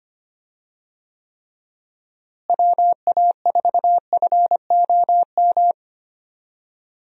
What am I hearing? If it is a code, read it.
Morse code, heard as a keyed tone, about 25 wpm, 700 Hz. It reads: WA4FOM
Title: USA Call Signs - 6 Characters - Repeated 3x - 25wpm